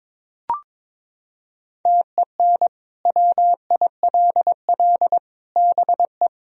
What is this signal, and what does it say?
Morse code, heard as a keyed tone, about 21 wpm, 700 Hz. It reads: TEN WILL BE
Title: Sentences from Top 1000 Words - Review Part 2 - 22wpm